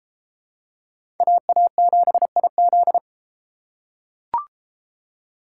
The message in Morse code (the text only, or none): AA7IZ